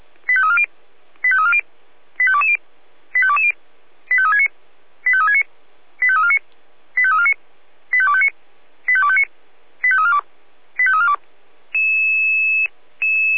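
Five-tone selective call (ZVEI), heard as quick runs of five tones, tones of about 70 ms. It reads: 85340 85340 85220 85220 85369 85369 85349 85349 85259 85259 85342 85342
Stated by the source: Five-tone ZVEI selective call 85252 received on an SDR.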